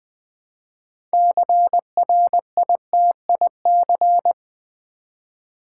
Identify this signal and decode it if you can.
Morse code, heard as a keyed tone, about 20 wpm, 700 Hz. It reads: CRITIC
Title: Build A Word - 2 Letters At A Time - 8-Letter Words - 20wpm